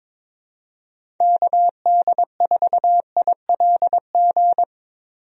Morse code, heard as a keyed tone, about 22 wpm, 700 Hz. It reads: KD4ILG
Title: USA Call Signs - 6 Characters - Repeated 2x - 22wpm